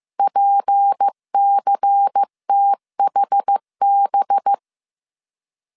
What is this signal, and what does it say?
Morse code, heard as a keyed tone, about 15 wpm, 800 Hz. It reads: PCTHB